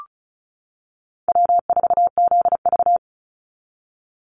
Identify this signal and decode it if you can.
Morse code, heard as a keyed tone, about 34 wpm, 700 Hz. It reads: W4ZV
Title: NA SOTA Chaser Callsigns 35wpm